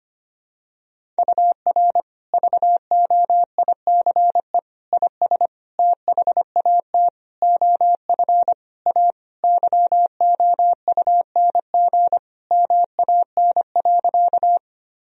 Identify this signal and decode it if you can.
Morse code, heard as a keyed tone, about 25 wpm, 700 Hz. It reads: UR VOICE IS THAT OF A YOUNG MAN.
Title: Sentences from Top 500 Words and Abbreviations 25wpm